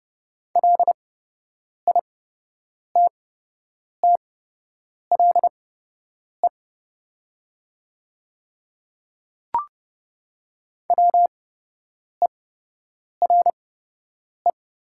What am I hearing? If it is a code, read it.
Morse code, heard as a keyed tone, about 30 wpm, spaced as for 8 wpm, 700 Hz. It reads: LITTLE WERE
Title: 8x Character Spacing - Top 300 Words - 30wpm